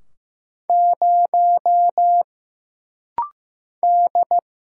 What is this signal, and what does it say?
Morse code, heard as a keyed tone, about 15 wpm, 700 Hz. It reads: 0 D
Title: Single Letter-Number Alt - Rapid-Fire 15wpm